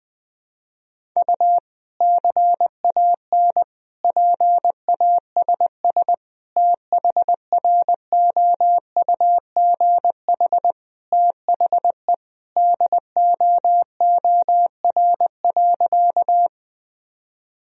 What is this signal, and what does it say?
Morse code, heard as a keyed tone, about 21 wpm, 700 Hz. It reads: U CAN PASS THROUGH THE DOOR.
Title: Sentences from Top 400 Words and Abbreviations 20wpm